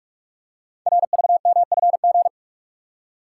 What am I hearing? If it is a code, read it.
Morse code, heard as a keyed tone, about 44 wpm, 700 Hz. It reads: RVMFG
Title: Code Groups - 5 Letter:Numbers - 45wpm